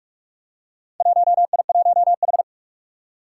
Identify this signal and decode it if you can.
Morse code, heard as a keyed tone, about 46 wpm, 700 Hz. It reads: 1I1H
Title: Four Challenging Letter-Number Combinations 45wpm